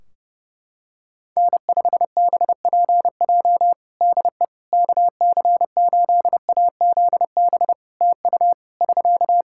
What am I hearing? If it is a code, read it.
Morse code, heard as a keyed tone, about 30 wpm, 700 Hz. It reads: N5BPJ DE KC8AZB TU <SK>
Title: K1USN SST QSO Practice 30wpm